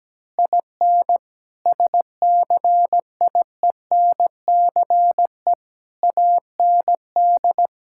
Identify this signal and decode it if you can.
Morse code, heard as a keyed tone, about 17 wpm, 700 Hz. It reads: IN SCIENCE AND
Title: Sentences from Top 1000 Words - Review Part 6 - 17wpm